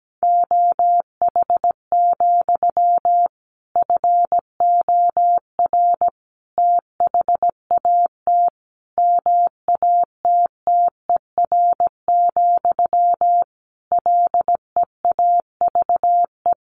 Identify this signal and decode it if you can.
Morse code, heard as a keyed tone, about 17 wpm, 700 Hz. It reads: OH, FOR THAT MATTER, LEAVE
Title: Sentences from Top 1400 Words 17wpm